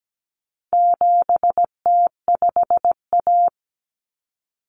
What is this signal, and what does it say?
Morse code, heard as a keyed tone, about 17 wpm, 700 Hz. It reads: 7T5A
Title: Common Cut Numbers - 4-Digit 17wpm